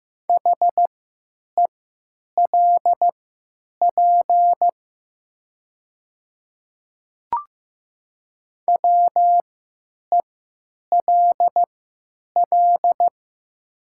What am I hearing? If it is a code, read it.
Morse code, heard as a keyed tone, about 15 wpm, spaced as for 9 wpm, 700 Hz. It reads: HELP WELL
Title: 3x Character Spacing - Top 200 Words - 15wpm